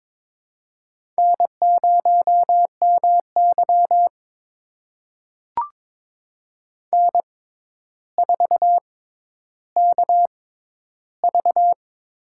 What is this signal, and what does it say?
Morse code, heard as a keyed tone, about 22 wpm, spaced as for 11 wpm, 700 Hz. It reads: N0MY N4KV
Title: USA Call Signs - 4 Characters - 6x Character Spacing - 22wpm